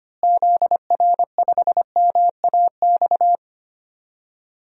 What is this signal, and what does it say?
Morse code, heard as a keyed tone, about 25 wpm, 700 Hz. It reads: ZR5MAX
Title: DXCC-100 Call Signs - 6 Characters - Repeated 3x - 25wpm